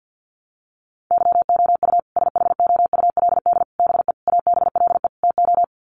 Morse code, heard as a keyed tone, about 49 wpm, 700 Hz. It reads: YOU SHOULD BE ABLE TO
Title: Sets of 5 Words Encore 50wpm